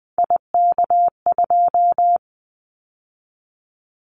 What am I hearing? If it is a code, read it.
Morse code, heard as a keyed tone, about 20 wpm, 700 Hz. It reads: IK2
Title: Callsign Prefixes for Top 25 DXCC - 20wpm